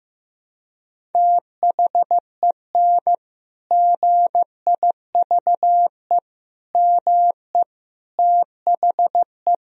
Morse code, heard as a keyed tone, about 15 wpm, 700 Hz. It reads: THEN GIVE ME THE
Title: Sentences from Top 1000 Words - Review Part 1 - 15wpm